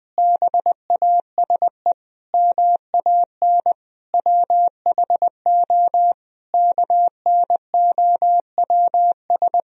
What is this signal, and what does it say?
Morse code, heard as a keyed tone, about 21 wpm, 700 Hz. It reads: BASE MAN WHO KNOWS
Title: Sentences from Top 1000 Words - Review Part 1 - 20wpm